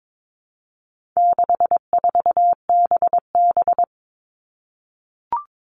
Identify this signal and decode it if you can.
Morse code, heard as a keyed tone, about 22 wpm, 700 Hz. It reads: 64BB
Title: All Cut Numbers - 4-Digit 22wpm